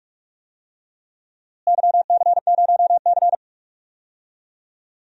Morse code, heard as a keyed tone, about 44 wpm, 700 Hz. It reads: YC0C
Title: DXCC-100 Call Signs - 4 Characters - Repeated 2x - 45wpm